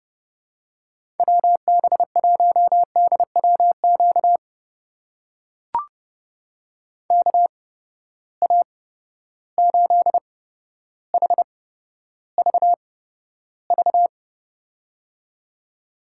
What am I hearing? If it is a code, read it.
Morse code, heard as a keyed tone, about 30 wpm, spaced as for 13 wpm, 700 Hz. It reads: WB1DWQ KA8HVV
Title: USA Call Signs - 6 Characters - 8x Character Spacing - 30wpm